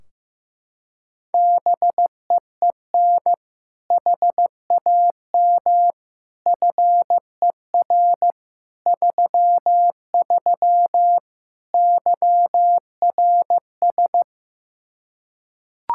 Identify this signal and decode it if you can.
Morse code, heard as a keyed tone, about 15 wpm, 700 Hz. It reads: BEEN HAM FER 33 YRS
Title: Practice QSO - Exchange 3 - More Info - 15wpm